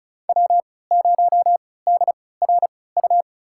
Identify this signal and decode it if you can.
Morse code, heard as a keyed tone, about 36 wpm, spaced as for 20 wpm, 700 Hz. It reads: W0DRU
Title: USA Call Signs - 5 Characters - 3x Character Spacing - 35wpm